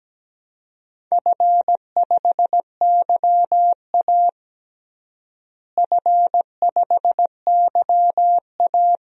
Morse code, heard as a keyed tone, about 17 wpm, 700 Hz. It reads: F5YA F5YA
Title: DXCC-100 Call Signs - 4 Characters - Repeated 2x - 17wpm